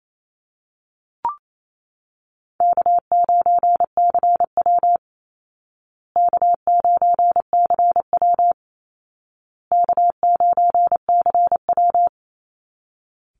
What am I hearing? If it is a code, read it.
Morse code, heard as a keyed tone, about 28 wpm, 700 Hz. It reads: K9CW K9CW K9CW
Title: Easy Call Signs Repeated 3x - 28wpm